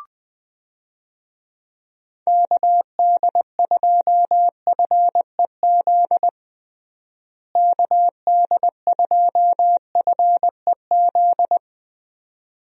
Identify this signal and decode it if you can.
Morse code, heard as a keyed tone, about 20 wpm, 700 Hz. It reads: KD2FEZ KD2FEZ
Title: USA Call Signs - 6 Characters - Repeated 2x - 20wpm